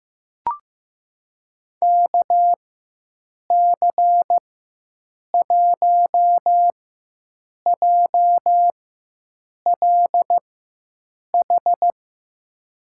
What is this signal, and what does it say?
Morse code, heard as a keyed tone, about 15 wpm, spaced as for 7 wpm, 700 Hz. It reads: KC1JLH
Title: USA Call Signs - 6 Characters - 4x Character Spacing - 15wpm